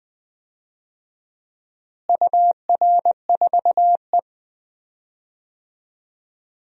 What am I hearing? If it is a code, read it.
Morse code, heard as a keyed tone, about 20 wpm, 700 Hz. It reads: UR4E
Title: DXCC-100 Call Signs - 4 Characters - Repeated 2x - 20wpm